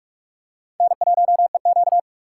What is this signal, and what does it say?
Morse code, heard as a keyed tone, about 42 wpm, 700 Hz. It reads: N1EQ
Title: USA Call Signs - 3x Character Spacing - 45wpm